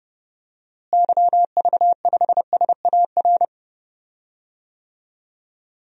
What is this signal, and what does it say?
Morse code, heard as a keyed tone, about 31 wpm, 700 Hz. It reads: YV5SAR
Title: DXCC-100 Call Signs - 6 Characters - Repeated 3x - 30wpm